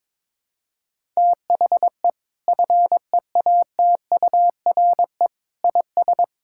Morse code, heard as a keyed tone, about 22 wpm, 700 Hz. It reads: THE FEATURE IS
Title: Sentences from Top 1000 Words - Review Part 5 - 22wpm